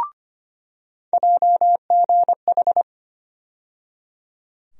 Morse code, heard as a keyed tone, about 25 wpm, 700 Hz. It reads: JGH
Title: USA Call Sign Postfixes - 3 Letters 25wpm